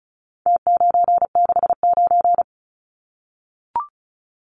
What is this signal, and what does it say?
Morse code, heard as a keyed tone, about 34 wpm, 700 Hz. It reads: T969
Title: Numbers - Contest Style - 4-Digit 35wpm